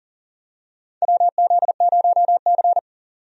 Morse code, heard as a keyed tone, about 40 wpm, 700 Hz. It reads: WZ0C
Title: USA Call Signs - 4 Characters - 40wpm